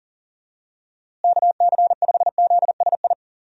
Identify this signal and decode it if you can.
Morse code, heard as a keyed tone, about 40 wpm, 700 Hz. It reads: KC5ZSI
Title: USA Call Signs - 6 Characters - Repeated 2x - 40wpm